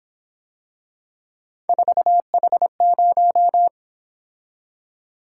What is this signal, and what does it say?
Morse code, heard as a keyed tone, about 26 wpm, 700 Hz. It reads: 4H0
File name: Speed-Racing - Three Letter-Number Combinations - 22wpm